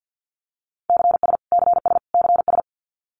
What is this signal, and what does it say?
Morse code, heard as a keyed tone, about 50 wpm, 700 Hz. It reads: CS CS CS
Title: Abbreviations Repeated 3x - 50wpm